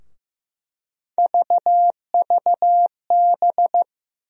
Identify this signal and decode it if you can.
Morse code, heard as a keyed tone, about 15 wpm, 700 Hz. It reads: VVB
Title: USA Call Sign Postfixes - 3 Letters 15wpm